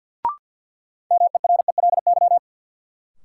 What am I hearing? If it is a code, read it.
Morse code, heard as a keyed tone, about 50 wpm, 700 Hz. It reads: MERELY
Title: Sentences from Top 1400 Words - New Word Drill Alt - 50wpm